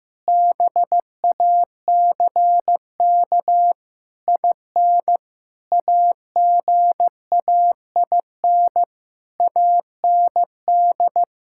Morse code, heard as a keyed tone, about 16 wpm, 700 Hz. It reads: BACK IN AGAIN AND
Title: Pangram Challenge v2 - 15wpm